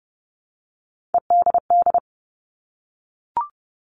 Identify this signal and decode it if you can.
Morse code, heard as a keyed tone, about 30 wpm, 700 Hz. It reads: EDD
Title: All Cut Numbers - 3-Digit 30wpm